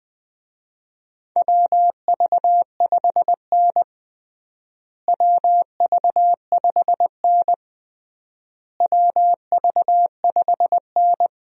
Morse code, heard as a keyed tone, about 20 wpm, 700 Hz. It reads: WV5N WV5N WV5N
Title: USA Call Signs - 4 Characters - Repeated 3x - 20wpm